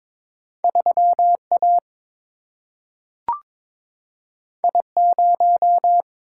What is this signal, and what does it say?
Morse code, heard as a keyed tone, about 22 wpm, 700 Hz. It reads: 3A I0
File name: practice / 22wpm / Two Letter-Number Combinations - 22wpm